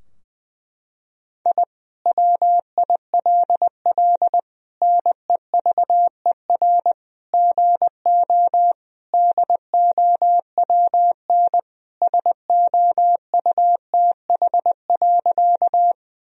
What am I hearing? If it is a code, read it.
Morse code, heard as a keyed tone, about 20 wpm, 700 Hz. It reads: I WILL NEVER GO DOWN SOUTH.
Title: Sentences from Top 1000 Words - Review Part 2 - 20wpm